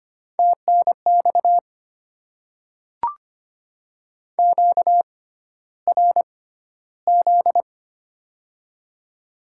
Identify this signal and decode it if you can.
Morse code, heard as a keyed tone, about 25 wpm, spaced as for 13 wpm, 700 Hz. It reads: TNX QRZ
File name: Top 90 QSO Elements - 6x Character Spacing - 25wpm